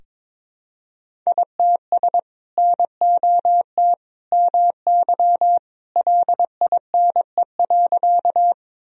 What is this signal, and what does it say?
Morse code, heard as a keyed tone, about 22 wpm, 700 Hz. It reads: ITS NOT MY LINE.
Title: Sentences from Top 1000 Words - Review Part 1 - 22wpm